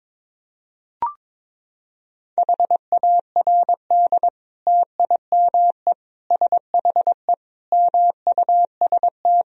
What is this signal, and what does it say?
Morse code, heard as a keyed tone, about 22 wpm, 700 Hz. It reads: HARD TIME SHE MUST
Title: Sentences from Top 1000 Words - Review Part 1 - 22wpm